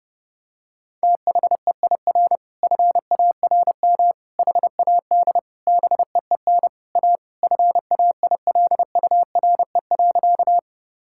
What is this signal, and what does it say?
Morse code, heard as a keyed tone, about 32 wpm, 700 Hz. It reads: THEIR FARM HAD BEEN A FAILURE.
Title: Sentences from Top 1000 Words - Review Part 6 - 30wpm